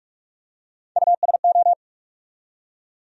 Morse code, heard as a keyed tone, about 48 wpm, 700 Hz. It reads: USO
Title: USA Call Sign Postfixes - 3 Letters 45wpm